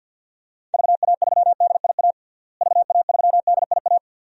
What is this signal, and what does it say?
Morse code, heard as a keyed tone, about 53 wpm, 700 Hz. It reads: VA3DIA VA3DIA
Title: DXCC-100 Call Signs - 6 Characters - Repeated 2x - 50wpm